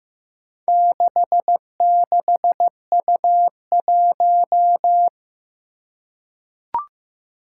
Morse code, heard as a keyed tone, about 15 wpm, 700 Hz. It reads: 66U1
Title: Four Challenging Letter-Number Combinations 15wpm